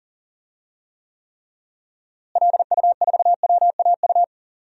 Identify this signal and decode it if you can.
Morse code, heard as a keyed tone, about 37 wpm, 700 Hz. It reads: LU4WAU
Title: DXCC-100 Call Signs - 6 Characters - Repeated 3x - 40wpm